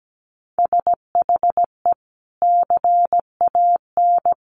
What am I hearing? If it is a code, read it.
Morse code, heard as a keyed tone, about 17 wpm, 700 Hz. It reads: SHE CAN
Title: Sentences from Top 700 Words 17wpm